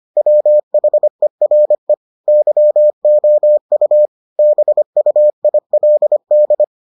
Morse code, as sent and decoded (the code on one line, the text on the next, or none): .-- .... .-. / -.-- --- ..- / -... ..- .. .-.. -..
WHERE YOU BUILD